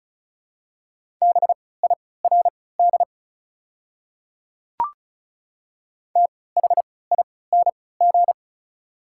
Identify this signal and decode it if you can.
Morse code, heard as a keyed tone, about 34 wpm, spaced as for 20 wpm, 690 Hz. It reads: BIRD THING
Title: 3x Character Spacing - Top 500 Words - 35wpm